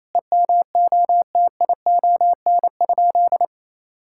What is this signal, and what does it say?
Morse code, heard as a keyed tone, about 28 wpm, 700 Hz. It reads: EMOTION?
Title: Sentences from Top 1600 Words 28wpm